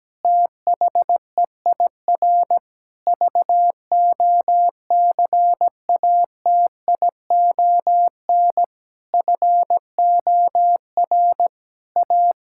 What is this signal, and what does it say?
Morse code, heard as a keyed tone, about 17 wpm, 700 Hz. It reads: THEIR VOCATION FOR A
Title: Pangram Challenge v2 - 17wpm